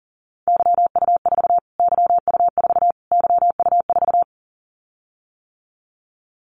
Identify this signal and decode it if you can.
Morse code, heard as a keyed tone, about 40 wpm, 700 Hz. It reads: YU4 YU4 YU4
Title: Callsign Prefixes for Top 25 DXCC - Repeated 3x - 40wpm